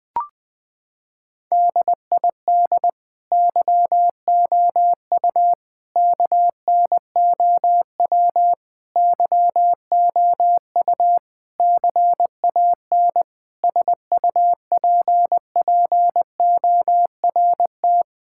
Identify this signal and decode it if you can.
Morse code, heard as a keyed tone, about 20 wpm, 700 Hz. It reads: DID YOU KNOW YOU CAN SUPPORT
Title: Sets of 9 Words - 20wpm